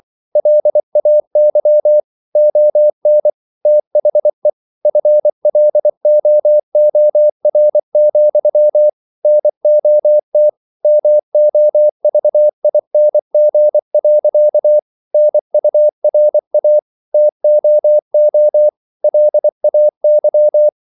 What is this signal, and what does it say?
Morse code, heard as a keyed tone, about 24 wpm, 600 Hz. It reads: LAY ON THE FLOOR, NOT MOVING. NURA TOO LAY